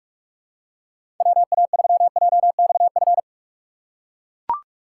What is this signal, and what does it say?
Morse code, heard as a keyed tone, about 44 wpm, 700 Hz. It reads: WA3JXF
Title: USA Call Signs - 6 Characters - 6x Character Spacing - 45wpm